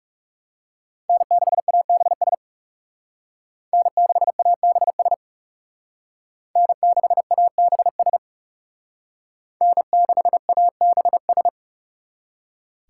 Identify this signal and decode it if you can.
Morse code, heard as a keyed tone, about 37 wpm, 700 Hz. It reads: N6ABS N6ABS N6ABS N6ABS
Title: Speed Racing - USA 1x3 Call Signs 30wpm